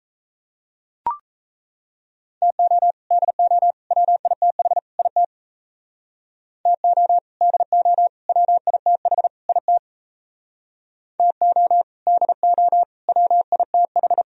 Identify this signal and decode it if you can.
Morse code, heard as a keyed tone, about 33 wpm, 700 Hz. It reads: TO DO WITH IT TO DO WITH IT TO DO WITH
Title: Speed-Racing - Sets of 4 Words 28wpm